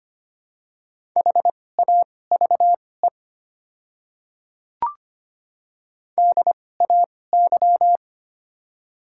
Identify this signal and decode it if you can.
Morse code, heard as a keyed tone, about 25 wpm, spaced as for 18 wpm, 690 Hz. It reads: HAVE DAY
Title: 2x Character Spacing - Top 100 Words - 25wpm